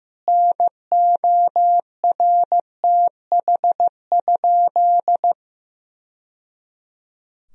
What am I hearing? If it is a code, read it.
Morse code, heard as a keyed tone, about 15 wpm, 700 Hz. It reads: NORTH?
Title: Sentences from Top 1000 Words - Review Part 2 - 15wpm